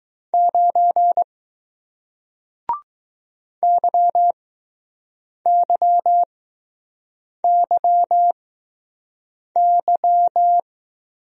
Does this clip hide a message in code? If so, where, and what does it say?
Yes: Morse code from 0.3 to 10.6 s, heard as a keyed tone, about 20 wpm, spaced as for 6 wpm, 700 Hz. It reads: 9 YYYY